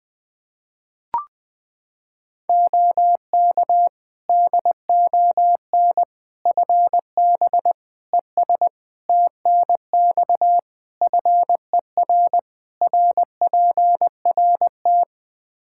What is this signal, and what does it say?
Morse code, heard as a keyed tone, about 21 wpm, 700 Hz. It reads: OK DON FB ES TNX FER RPRT
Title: Practice QSO - Exchange 2 - Info - 20wpm